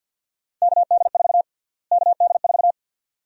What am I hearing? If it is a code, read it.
Morse code, heard as a keyed tone, about 49 wpm, 700 Hz. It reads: KD4 KD4
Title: USA Call Sign Prefixes - Repeated 2x - 50wpm